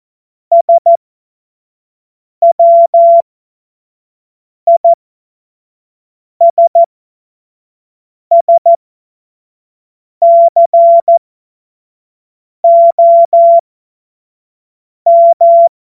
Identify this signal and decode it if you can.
Morse code, heard as a keyed tone, about 14 wpm, spaced as for 5 wpm, 680 Hz. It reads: SWISSCOM
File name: Morsekurs / Witze / Witz 20 Februar 2024 25 F70